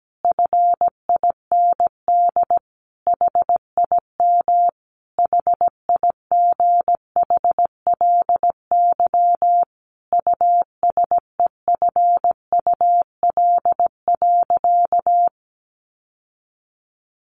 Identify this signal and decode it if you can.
Morse code, heard as a keyed tone, about 17 wpm, 700 Hz. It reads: FIND HIM HIGHLY USEFUL.
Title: Sentences from Top 1100 Words 17wpm